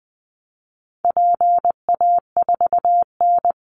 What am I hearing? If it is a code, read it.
Morse code, heard as a keyed tone, about 20 wpm, 700 Hz. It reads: PA4N
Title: Call Signs Repeated 2x - 20wpm